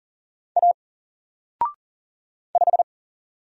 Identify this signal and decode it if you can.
Morse code, heard as a keyed tone, about 39 wpm, 700 Hz. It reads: A 5
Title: Single Letter-Number 40wpm